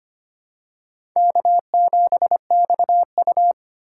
Morse code, heard as a keyed tone, about 25 wpm, 700 Hz. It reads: K7XU
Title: USA Call Signs - 3x Character Spacing - 25wpm